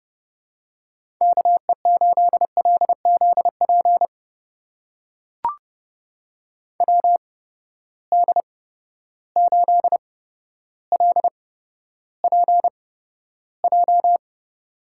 Morse code, heard as a keyed tone, about 30 wpm, spaced as for 13 wpm, 700 Hz. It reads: KE8LZP WD8LPJ
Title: USA Call Signs - 6 Characters - 8x Character Spacing - 30wpm